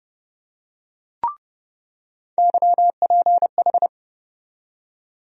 Morse code, heard as a keyed tone, about 30 wpm, 700 Hz. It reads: YPH
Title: USA Call Sign Postfixes - 3 Letters 30wpm